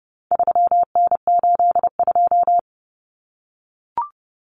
Morse code, heard as a keyed tone, about 30 wpm, 700 Hz. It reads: 3N82
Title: Common Cut Numbers - 4-Digit 30wpm